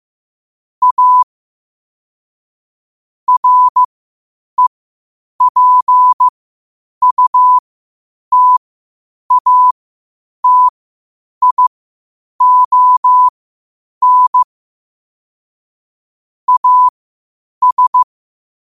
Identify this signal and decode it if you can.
Morse code, heard as a keyed tone, about 15 wpm, spaced as for 8 wpm, 1000 Hz. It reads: A REPUTATION AS
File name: qotdfiles / 8wpm